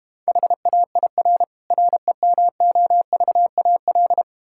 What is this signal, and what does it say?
Morse code, heard as a keyed tone, about 32 wpm, 700 Hz. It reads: HAIR REMOVAL